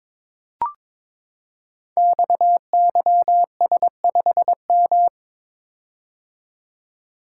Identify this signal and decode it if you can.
Morse code, heard as a keyed tone, about 22 wpm, 700 Hz. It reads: XYS5M
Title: Code Groups - 5 Letter:Numbers - 22wpm